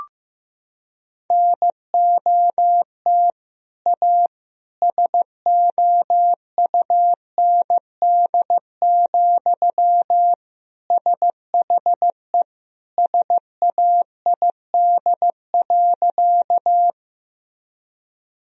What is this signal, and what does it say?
Morse code, heard as a keyed tone, about 15 wpm, 700 Hz. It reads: NOT A SOUND, SHE SAID.
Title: Sentences from Top 1000 Words - Review Part 1 - 15wpm